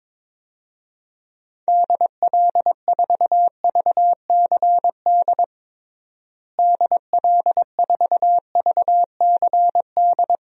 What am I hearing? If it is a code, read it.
Morse code, heard as a keyed tone, about 22 wpm, 700 Hz. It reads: DL4VCD DL4VCD
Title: DXCC-100 Call Signs - 6 Characters - Repeated 2x - 22wpm